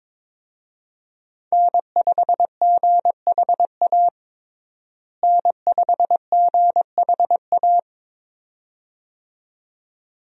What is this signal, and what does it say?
Morse code, heard as a keyed tone, about 22 wpm, 700 Hz. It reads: N5GHA N5GHA
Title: USA Call Signs - 5 Characters - Repeated 2x - 22wpm